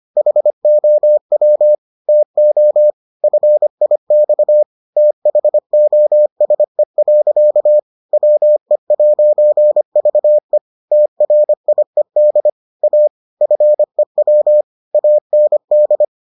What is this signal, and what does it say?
Morse code, heard as a keyed tone, about 25 wpm, 600 Hz. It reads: HOW TO FIX THOSE. WE'VE TRIED A FEW AND